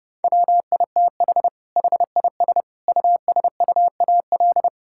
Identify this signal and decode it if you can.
Morse code, heard as a keyed tone, about 30 wpm, 700 Hz. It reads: WITH HIS USUAL
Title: Sentences from Top 1000 Words - Review Part 6 - 30wpm